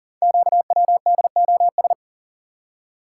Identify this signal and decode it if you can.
Morse code, heard as a keyed tone, about 38 wpm, 690 Hz. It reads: QWDOS
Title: Code Groups - 5 Letters - 40wpm